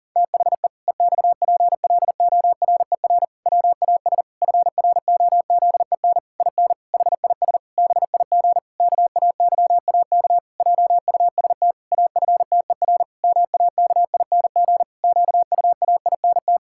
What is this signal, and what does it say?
Morse code, heard as a keyed tone, about 40 wpm, 700 Hz. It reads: THE EXPLORER WAS FROZEN IN HIS BIG KAYAK JUST AFTER MAKING QUAINT
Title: Pangram Challenge v2 - 40wpm